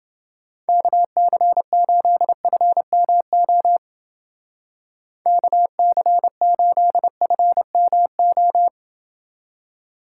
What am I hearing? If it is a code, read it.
Morse code, heard as a keyed tone, about 28 wpm, 700 Hz. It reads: KC8FMO KC8FMO